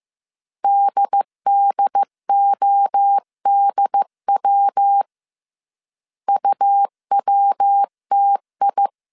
Morse code, heard as a keyed tone, about 15 wpm, 800 Hz. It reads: DDODW UWTI
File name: CW / DET / 1 3 1500C S C 14wpm 2458